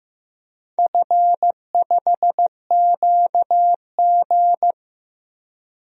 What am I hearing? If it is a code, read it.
Morse code, heard as a keyed tone, about 15 wpm, 700 Hz. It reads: F5QG
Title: DXCC-100 Call Signs - 4 Characters - Repeated 2x - 15wpm